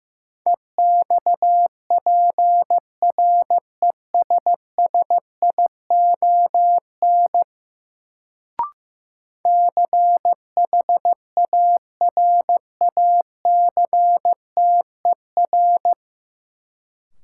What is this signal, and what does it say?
Morse code, heard as a keyed tone, about 15 wpm, 700 Hz. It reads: EXPRESSION CHARACTER
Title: Sentences from Top 1200 Words - New Word Drill - 15wpm